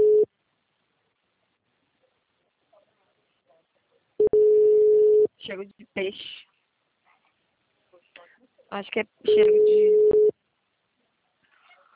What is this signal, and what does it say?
Ringback tone (Europe, 425 Hz), with speech.
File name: out- -2005- - - .49823